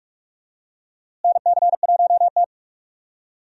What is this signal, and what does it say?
Morse code, heard as a keyed tone, about 41 wpm, 700 Hz. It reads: NC1T
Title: USA Call Signs - 4 Characters - Repeated 3x - 45wpm